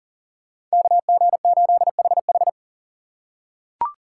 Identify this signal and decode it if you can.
Morse code, heard as a keyed tone, about 40 wpm, 700 Hz. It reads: KG8HH
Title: USA Call Signs - 5 Characters - 4x Character Spacing - 40wpm